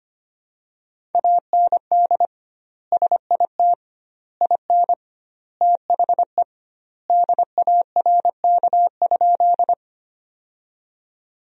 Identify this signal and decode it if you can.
Morse code, heard as a keyed tone, about 25 wpm, 700 Hz. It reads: AND SIT IN THE DARK?
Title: 2x Word-Spacing - Sentences from Top 500 Words - 25wpm